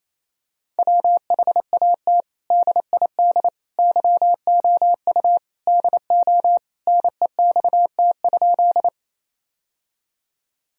Morse code, heard as a keyed tone, about 28 wpm, 700 Hz. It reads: WHAT DID YOU DO NEXT?
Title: Sentences from Top 1000 Words - Review Part 1 - 28wpm